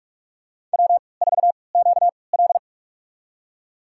Morse code, heard as a keyed tone, about 41 wpm, spaced as for 25 wpm, 700 Hz. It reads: W4QL
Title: USA Call Signs - 4 Characters - 3x Character Spacing - 45wpm